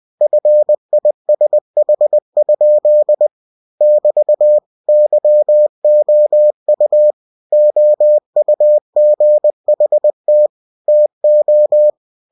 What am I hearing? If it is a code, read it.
Morse code, heard as a keyed tone, about 20 wpm, 600 Hz. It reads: FISH? <BT> YOU OUGHT TO
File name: cw-books / winnie-the-pooh / winnie-20wpm-page-0069